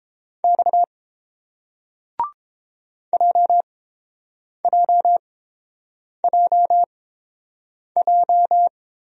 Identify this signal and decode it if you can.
Morse code, heard as a keyed tone, about 30 wpm, spaced as for 7 wpm, 700 Hz. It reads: X JJJJ